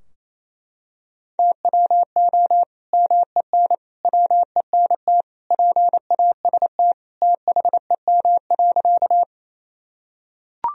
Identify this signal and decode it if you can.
Morse code, heard as a keyed tone, about 28 wpm, 700 Hz. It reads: TWO MEN WENT PAST THEM.
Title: Sentences from Top 1000 Words - Review Part 3 - 28wpm